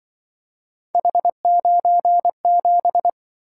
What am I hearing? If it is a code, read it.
Morse code, heard as a keyed tone, about 25 wpm, 700 Hz. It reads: H97
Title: Speed-Racing - Three Letter-Number Combinations - 20wpm